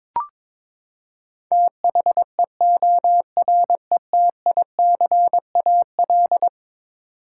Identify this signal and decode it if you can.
Morse code, heard as a keyed tone, about 22 wpm, 700 Hz. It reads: THEORETICAL
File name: practice / 22wpm / Eleven Letter Words - 22wpm